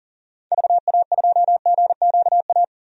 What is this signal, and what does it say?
Morse code, heard as a keyed tone, about 38 wpm, 700 Hz. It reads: VA2GQA
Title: DXCC-100 Call Signs - 6 Characters - 40wpm